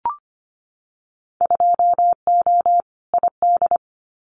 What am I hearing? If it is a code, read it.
Morse code, heard as a keyed tone, about 25 wpm, 700 Hz. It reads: 2O ID
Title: Winter Field Day Exchanges 25wpm